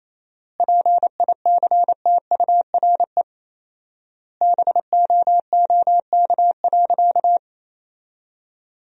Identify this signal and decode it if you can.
Morse code, heard as a keyed tone, about 28 wpm, 700 Hz. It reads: PICTURE BOOK.
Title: 4x Word-Spacing - Sentences from Top 500 Words - 28wpm